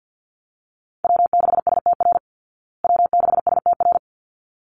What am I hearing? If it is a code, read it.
Morse code, heard as a keyed tone, about 50 wpm, 700 Hz. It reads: W6STR W6STR
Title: Call Signs Repeated 3x - 50wpm